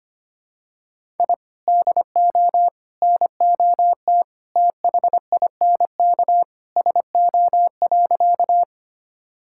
Morse code, heard as a keyed tone, about 25 wpm, 700 Hz. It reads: I DO NOT THINK SO.
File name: practice / 25wpm / Sentences with 3-Word Phrases - 25wpm